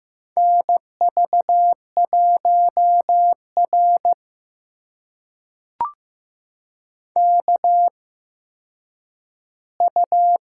Morse code, heard as a keyed tone, about 15 wpm, spaced as for 9 wpm, 700 Hz. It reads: NV1R KU
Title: USA Call Signs - 4 Characters - 8x Character Spacing - 15wpm